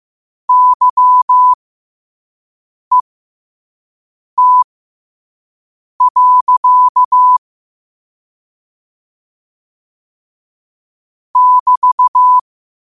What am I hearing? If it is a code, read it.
Morse code, heard as a keyed tone, about 15 wpm, spaced as for 5 wpm, 1000 Hz. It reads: YET. <BT>
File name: qotdfiles / 5wpm